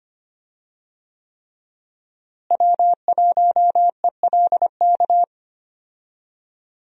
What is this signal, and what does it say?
Morse code, heard as a keyed tone, about 25 wpm, 700 Hz. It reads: W1ELK
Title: USA Call Signs - 5 Characters - Repeated 3x - 25wpm